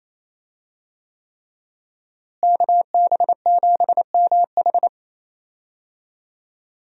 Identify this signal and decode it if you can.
Morse code, heard as a keyed tone, about 28 wpm, 700 Hz. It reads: KB7MH
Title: USA Call Signs - 5 Characters - Repeated 3x - 28wpm